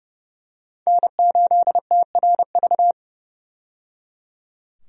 Morse code, heard as a keyed tone, about 30 wpm, 700 Hz. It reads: N8TRV
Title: USA Call Signs - 5 Characters - 30wpm